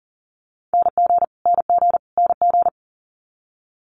Morse code, heard as a keyed tone, about 41 wpm, 690 Hz. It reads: NG NG NG
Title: Multi-letter Phonemes Repeated 3x 40wpm